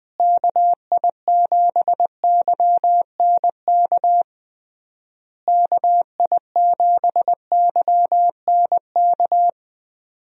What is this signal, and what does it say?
Morse code, heard as a keyed tone, about 21 wpm, 700 Hz. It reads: KI7YNK KI7YNK
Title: USA Call Signs - 6 Characters - Repeated 3x - 20wpm